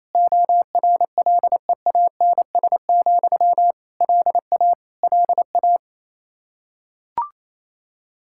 Morse code, heard as a keyed tone, about 28 wpm, 700 Hz. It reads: ORLEANS, LA LA
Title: Practice QSO - Exchange 1 - Beginning - 28wpm